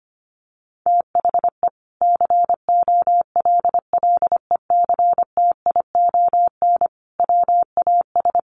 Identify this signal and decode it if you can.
Morse code, heard as a keyed tone, about 25 wpm, 700 Hz. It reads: THE COLLECTION WAS
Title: Sentences from Top 1200 Words 25wpm